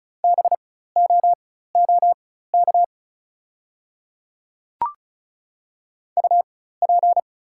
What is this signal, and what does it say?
Morse code, heard as a keyed tone, about 36 wpm, spaced as for 16 wpm, 690 Hz. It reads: BOOK UP
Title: 4x Character Spacing - Top 300 Words - 35wpm